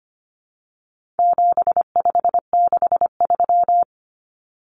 Morse code, heard as a keyed tone, about 25 wpm, 700 Hz. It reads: 7563
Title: Numbers - 4-Digit 25wpm